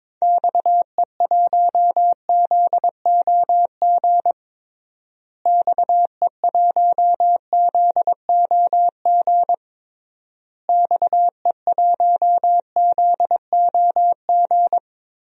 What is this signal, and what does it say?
Morse code, heard as a keyed tone, about 22 wpm, 700 Hz. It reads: XE1ZOG XE1ZOG XE1ZOG
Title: DXCC-100 Call Signs - 6 Characters - Repeated 3x - 22wpm